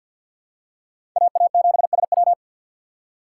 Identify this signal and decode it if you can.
Morse code, heard as a keyed tone, about 49 wpm, 700 Hz. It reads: AA7SW
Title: USA Call Signs - 5 Characters - 6x Character Spacing - 50wpm